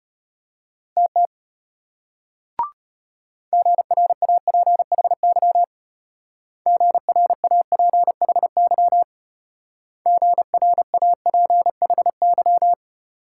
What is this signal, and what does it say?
Morse code, heard as a keyed tone, about 32 wpm, 700 Hz. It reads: TT GRAPHY GRAPHY GRAPHY